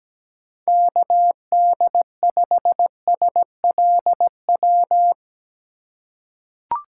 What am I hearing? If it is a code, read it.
Morse code, heard as a keyed tone, about 17 wpm, 700 Hz. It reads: KD5SLW
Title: USA Call Signs - 6 Characters - 4x Character Spacing - 17wpm